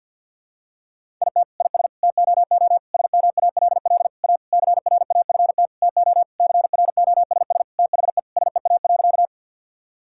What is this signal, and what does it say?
Morse code, heard as a keyed tone, about 52 wpm, 700 Hz. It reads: IT IS TOO SMALL A CRAFT TO CROSS THE SEA.
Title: Sentences from Top 1000 Words - Review Part 6 - 50wpm